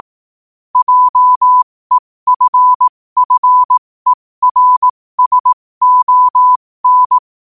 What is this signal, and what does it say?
Morse code, heard as a keyed tone, about 18 wpm, spaced as for 15 wpm, 1000 Hz. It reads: JEFFERSON